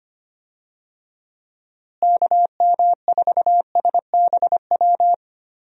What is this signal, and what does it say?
Morse code, heard as a keyed tone, about 25 wpm, 700 Hz. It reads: KM4SBW
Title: USA Call Signs - 6 Characters - Repeated 3x - 25wpm